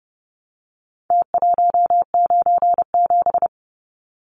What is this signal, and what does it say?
Morse code, heard as a keyed tone, about 30 wpm, 700 Hz. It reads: T197
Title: Numbers - Contest Style - 4-Digit 30wpm